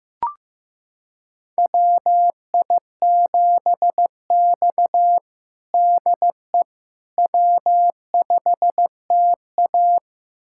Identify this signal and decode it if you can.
Morse code, heard as a keyed tone, about 16 wpm, 700 Hz. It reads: WI7X DE W5TA
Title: USA Callsign DE Callsign - 4 Characters - 15wpm